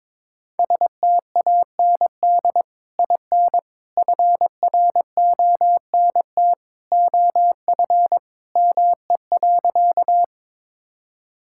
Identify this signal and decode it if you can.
Morse code, heard as a keyed tone, about 22 wpm, 700 Hz. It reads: STAND IN FRONT OF ME.
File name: practice / 22wpm / Sentences with 2-Word Phrases - 22wpm